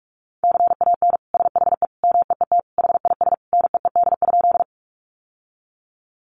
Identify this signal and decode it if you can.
Morse code, heard as a keyed tone, about 44 wpm, 700 Hz. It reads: CAN SHE MEET HIS NEED?
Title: Sentences from Top 1100 Words 45wpm